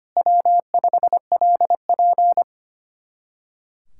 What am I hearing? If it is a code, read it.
Morse code, heard as a keyed tone, about 31 wpm, 700 Hz. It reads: W5LP
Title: Speed-Racing Common QSO Phrases - 25wpm